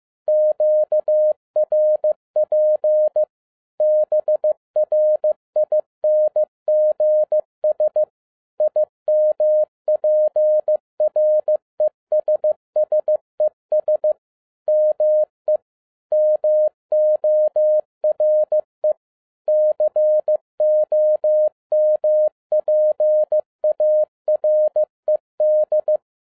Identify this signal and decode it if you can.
Morse code, heard as a keyed tone, about 15 wpm, 610 Hz. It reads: QRP BRINGS IMPRESSES ME MORE COMPARED
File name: 2011-07-mp3 / n22 QRP OTAKU 15wpm